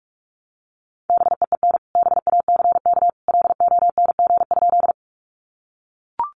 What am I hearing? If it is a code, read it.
Morse code, heard as a keyed tone, about 44 wpm, 700 Hz. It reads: BEEN BACK LONG?
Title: Sentences from Top 200 Words 45wpm